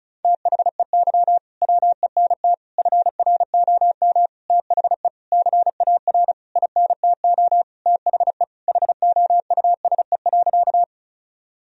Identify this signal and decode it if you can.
Morse code, heard as a keyed tone, about 36 wpm, 700 Hz. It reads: THEY WENT FROM THE CAR INTO THE HOUSE.
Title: Sentences from Top 1000 Words - Review Part 1 - 35wpm